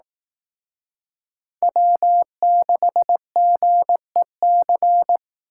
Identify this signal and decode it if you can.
Morse code, heard as a keyed tone, about 18 wpm, 700 Hz. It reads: W6GEC
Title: Speed Racing - USA 1x3 Call Signs 15wpm